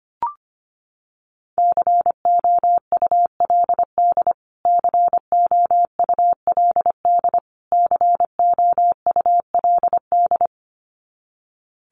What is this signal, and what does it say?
Morse code, heard as a keyed tone, about 25 wpm, 700 Hz. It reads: COULD COULD COULD
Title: Top 200 Words Repeated 25wpm